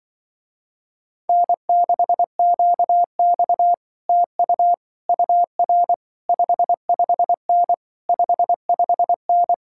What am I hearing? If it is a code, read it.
Morse code, heard as a keyed tone, about 24 wpm, 700 Hz. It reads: N6QX TU UR 55N 55N
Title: Speed-Racing Common QSO Phrases - 20wpm